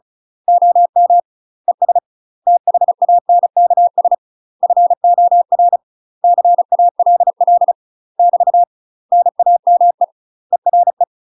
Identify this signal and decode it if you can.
Morse code, heard as a keyed tone, about 34 wpm, 700 Hz. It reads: OM ES THANKS FOR CALL <BT> NAME ERE
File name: UK MASTER QSOs TRAINING / 35WPM / UK MASTER QSOs 0021